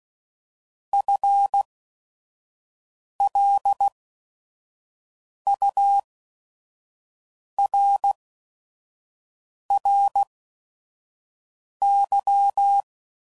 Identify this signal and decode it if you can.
Morse code, heard as a keyed tone, about 16 wpm, spaced as for 5 wpm, 800 Hz. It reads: FLURRY